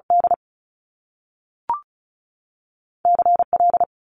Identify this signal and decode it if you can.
Morse code, heard as a keyed tone, about 36 wpm, 700 Hz. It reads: D C L
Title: Top Two Letter Combinations in Words 35wpm